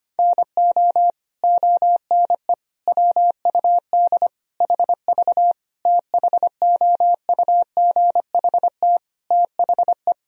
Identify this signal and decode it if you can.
Morse code, heard as a keyed tone, about 25 wpm, 700 Hz. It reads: NO ONE WUD HV THOUGHT THE
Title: Sentences from Top 300 Words and Abbreviations 25wpm